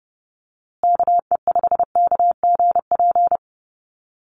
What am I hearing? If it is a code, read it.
Morse code, heard as a keyed tone, about 30 wpm, 700 Hz. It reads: KE5KGP
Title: USA Call Signs Repeated 3x 30wpm